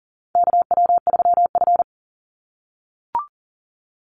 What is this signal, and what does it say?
Morse code, heard as a keyed tone, about 42 wpm, 700 Hz. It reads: KW3F
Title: Call Signs - 40wpm